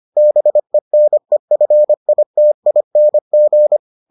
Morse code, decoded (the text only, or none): BENEFITING